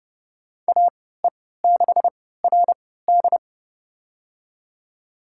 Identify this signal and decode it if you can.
Morse code, heard as a keyed tone, about 31 wpm, spaced as for 17 wpm, 700 Hz. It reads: AE6RD
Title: USA Call Signs - 5 Characters - 3x Character Spacing - 30wpm